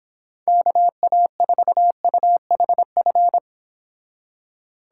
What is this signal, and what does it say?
Morse code, heard as a keyed tone, about 26 wpm, 700 Hz. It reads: KA4UHF
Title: Speed Racing - USA 2x3 Call Signs 22wpm